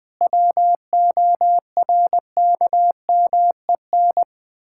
Morse code, heard as a keyed tone, about 20 wpm, 700 Hz. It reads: WORKMEN
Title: Pangram Challenge v2 - 20wpm